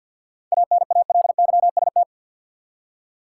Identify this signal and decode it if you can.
Morse code, heard as a keyed tone, about 47 wpm, 700 Hz. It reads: ANALYST
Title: Practice QSO - Exchange 3 - More Info - 50wpm